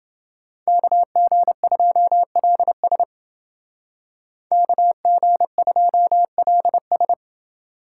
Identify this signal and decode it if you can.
Morse code, heard as a keyed tone, about 28 wpm, 700 Hz. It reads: KG2LS KG2LS